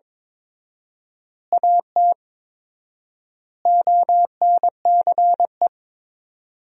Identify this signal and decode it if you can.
Morse code, heard as a keyed tone, about 22 wpm, 700 Hz. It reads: AT ONCE
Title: 4x Word-Spacing - Sentences from Top 300 Words - 22wpm